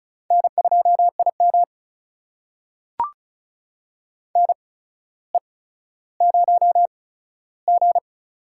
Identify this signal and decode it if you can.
Morse code, heard as a keyed tone, about 36 wpm, spaced as for 15 wpm, 700 Hz. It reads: N2IM NE0G
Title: USA Call Signs - 4 Characters - 8x Character Spacing - 35wpm